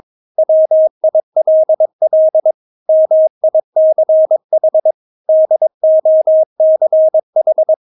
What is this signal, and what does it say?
Morse code, heard as a keyed tone, about 22 wpm, 630 Hz. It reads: WILL MICH DOCH
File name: Witz3 22wpm